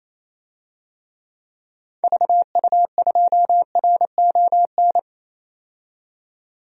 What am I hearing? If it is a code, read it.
Morse code, heard as a keyed tone, about 28 wpm, 700 Hz. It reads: VU2RON
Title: DXCC-100 Call Signs - 6 Characters - 28wpm